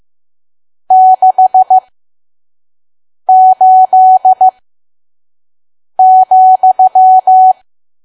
Morse code, heard as a keyed tone, about 15 wpm, spaced as for 5 wpm, 750 Hz. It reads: 68,
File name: Archive / 5 WPM files / PM